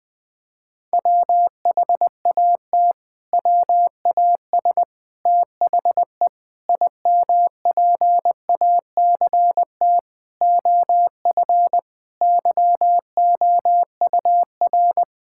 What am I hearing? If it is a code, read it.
Morse code, heard as a keyed tone, about 20 wpm, 700 Hz. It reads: WHAT WAS THE IMPACT OF YOUR
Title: Sentences from Top 1000 Words - Review Part 6 - 20wpm